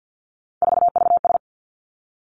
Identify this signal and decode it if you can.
Morse code, heard as a keyed tone, about 50 wpm, 700 Hz. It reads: 4VS
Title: Three Challenging Letter-Number Combinations 50wpm